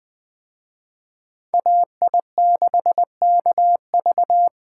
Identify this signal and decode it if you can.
Morse code, heard as a keyed tone, about 20 wpm, 700 Hz. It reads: AI6KV
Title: Speed Racing - USA 2x2 Call Signs 17wpm